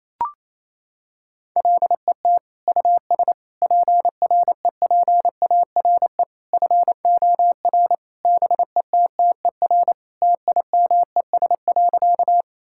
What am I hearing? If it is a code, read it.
Morse code, heard as a keyed tone, about 28 wpm, 700 Hz. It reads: LET US PREPARE FOR BETTER TIMES.
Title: Sentences from Top 1000 Words - Review Part 6 - 28wpm